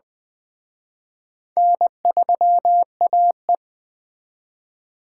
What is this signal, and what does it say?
Morse code, heard as a keyed tone, about 21 wpm, 700 Hz. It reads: N3AE
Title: Speed Racing - USA 1x2 Call Signs 17wpm